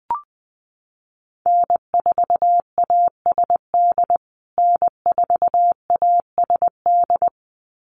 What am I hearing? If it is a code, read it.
Morse code, heard as a keyed tone, about 20 wpm, 700 Hz. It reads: N4ASD N4ASD
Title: USA Call Signs Repeated 2x 20wpm